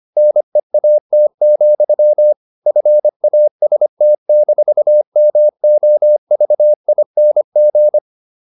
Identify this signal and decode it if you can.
Morse code, heard as a keyed tone, about 25 wpm, 600 Hz. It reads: NEAT, FAST-MOVING
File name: the old man and the sea part 1 25wpm